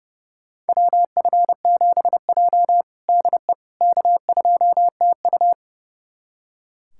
Morse code, heard as a keyed tone, about 30 wpm, 700 Hz. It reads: WF7J DE K2TU
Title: USA Callsign DE Callsign - 4 Characters - 30wpm